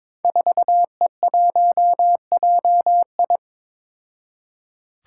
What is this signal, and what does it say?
Morse code, heard as a keyed tone, about 22 wpm, 700 Hz. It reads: 4E1JI
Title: Five Challenging Letter-Number Combinations 22wpm